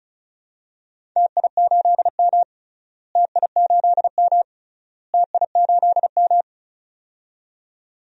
Morse code, heard as a keyed tone, about 34 wpm, 700 Hz. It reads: TI8M TI8M TI8M
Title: DXCC-100 Call Signs - 4 Characters - Repeated 3x - 35wpm